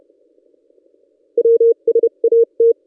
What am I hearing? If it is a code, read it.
Morse code, heard as a keyed tone, about 31 wpm, spaced as for 27 wpm, 460 Hz. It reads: WSAT
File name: contests / ss-cw-k5tr-op-n5rz -1600